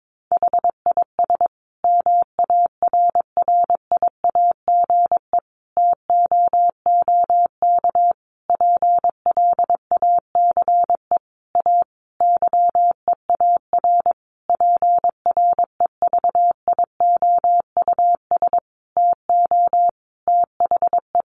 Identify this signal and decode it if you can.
Morse code, heard as a keyed tone, about 22 wpm, 700 Hz. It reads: HIS MARRIAGE TOOK PLACE A YEAR PREVIOUS TO THE
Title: Sentences from Top 1500 Words 22wpm